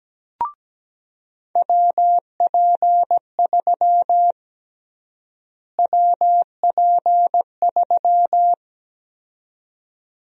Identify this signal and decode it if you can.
Morse code, heard as a keyed tone, about 17 wpm, 700 Hz. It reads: WP3 WP3
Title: USA Call Sign Prefixes - Repeated 2x - 17wpm